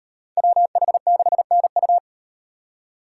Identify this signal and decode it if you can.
Morse code, heard as a keyed tone, about 38 wpm, 700 Hz. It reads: WH6NU